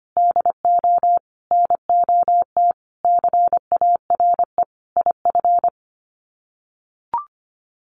Morse code, heard as a keyed tone, about 25 wpm, 700 Hz. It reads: DO NOT CARE IF
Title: Sets of 5 Words 25wpm